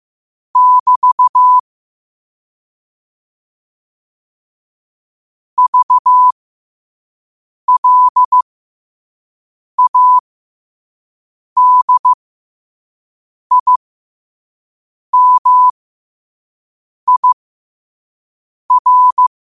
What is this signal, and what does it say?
Morse code, heard as a keyed tone, about 15 wpm, spaced as for 5 wpm, 1000 Hz. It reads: <BT> VLADIMIR
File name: qotdfiles / 5wpm